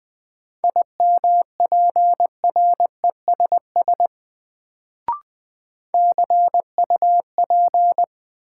Morse code, heard as a keyed tone, about 20 wpm, 700 Hz. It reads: IMPRESS CUP
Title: Sentences from Top 1600 Words - New Word Drill - 20wpm